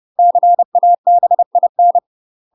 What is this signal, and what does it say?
Morse code, heard as a keyed tone, about 30 wpm, 700 Hz. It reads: CABIN
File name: UK MASTER QSOs TRAINING/30WPM